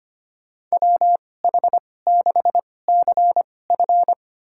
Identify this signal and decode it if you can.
Morse code, heard as a keyed tone, about 25 wpm, spaced as for 18 wpm, 700 Hz. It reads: WH6CF
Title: USA Call Signs - 2x Character Spacing - 25wpm